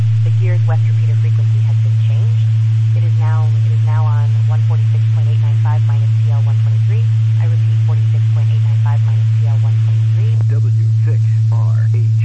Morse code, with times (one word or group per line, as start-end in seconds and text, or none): none